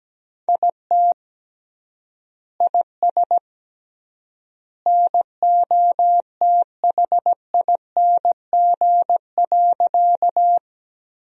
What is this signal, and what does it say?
Morse code, heard as a keyed tone, about 17 wpm, 700 Hz. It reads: IT IS NOTHING.
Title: 3x Word-Spacing - Sentences from Top 500 Words - 17wpm